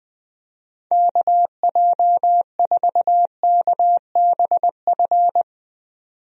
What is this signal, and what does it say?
Morse code, heard as a keyed tone, about 21 wpm, 700 Hz. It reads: KJ4KBF
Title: USA Call Signs - 6 Characters - 20wpm